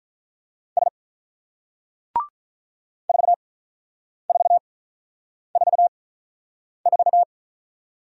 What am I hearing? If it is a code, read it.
Morse code, heard as a keyed tone, about 41 wpm, spaced as for 9 wpm, 700 Hz. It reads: S 4444
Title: Speed-Racing Single Letter-Number 35wpm